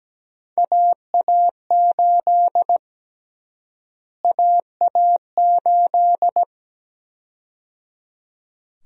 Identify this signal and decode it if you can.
Morse code, heard as a keyed tone, about 17 wpm, 700 Hz. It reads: AA8 AA8
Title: USA Call Sign Prefixes - Repeated 2x - 17wpm